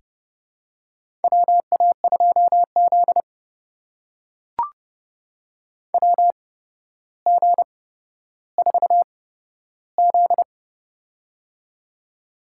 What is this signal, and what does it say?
Morse code, heard as a keyed tone, about 31 wpm, spaced as for 13 wpm, 700 Hz. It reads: WA2Z WG4Z
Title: USA Call Signs - 4 Characters - 8x Character Spacing - 30wpm